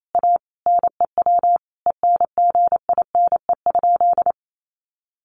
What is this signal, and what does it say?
Morse code, heard as a keyed tone, about 28 wpm, 700 Hz. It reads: A NEW ENGINE?
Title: Sentences from Top 1500 Words 28wpm